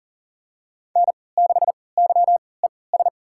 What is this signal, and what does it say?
Morse code, heard as a keyed tone, about 38 wpm, spaced as for 22 wpm, 690 Hz. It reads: N6YES
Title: USA Call Signs - 5 Characters - 3x Character Spacing - 40wpm